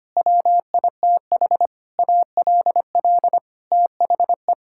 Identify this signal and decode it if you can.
Morse code, heard as a keyed tone, about 25 wpm, 700 Hz. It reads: WITH ALL THE
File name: Sets of 9 Words - 25wpm